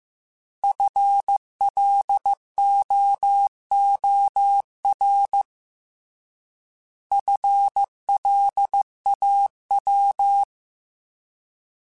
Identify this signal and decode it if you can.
Morse code, heard as a keyed tone, about 15 wpm, 800 Hz. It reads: FLOOR FLAW
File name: Alphabetical / Alphabetical mp3s / Fs @13